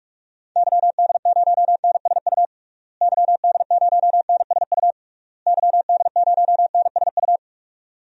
Morse code, heard as a keyed tone, about 41 wpm, 700 Hz. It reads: YD0NSU YD0NSU YD0NSU
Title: DXCC-100 Call Signs - 6 Characters - Repeated 3x - 45wpm